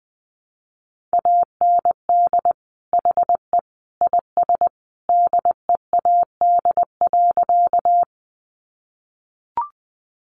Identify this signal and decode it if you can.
Morse code, heard as a keyed tone, about 20 wpm, 700 Hz. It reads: AND HE IS DEAD.
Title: Sentences from Top 1500 Words 20wpm